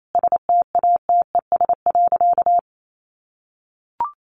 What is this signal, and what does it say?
Morse code, heard as a keyed tone, about 28 wpm, 700 Hz. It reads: STATES.
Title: A Sound Of Thunder 28wpm